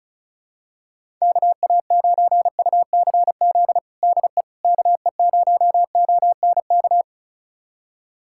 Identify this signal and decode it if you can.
Morse code, heard as a keyed tone, about 36 wpm, 700 Hz. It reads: KA9UCZ DE KE0ONK
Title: USA Callsign DE Callsign - 6 Characters - 35wpm